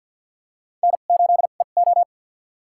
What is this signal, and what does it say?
Morse code, heard as a keyed tone, about 47 wpm, spaced as for 35 wpm, 700 Hz. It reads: N8EO
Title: USA Call Signs - 2x Character Spacing - 50wpm